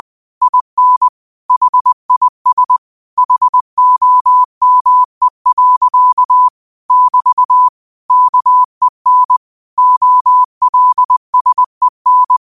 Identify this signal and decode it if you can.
Morse code, heard as a keyed tone, about 20 wpm, 1000 Hz. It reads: IN HIS HOME. <BT> KEN OLSEN